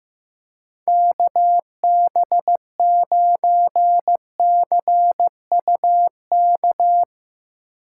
Morse code, heard as a keyed tone, about 15 wpm, 700 Hz. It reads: KB9CUK
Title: USA Call Signs - 6 Characters - 8x Character Spacing - 15wpm